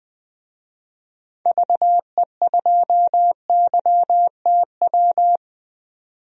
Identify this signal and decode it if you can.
Morse code, heard as a keyed tone, about 20 wpm, 700 Hz. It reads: VE2YTW
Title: DXCC-100 Call Signs - 6 Characters - 20wpm